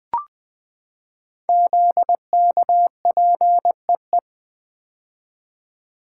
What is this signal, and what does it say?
Morse code, heard as a keyed tone, about 20 wpm, 700 Hz. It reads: ZKPEE
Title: Code Groups - 5 Letters - 20wpm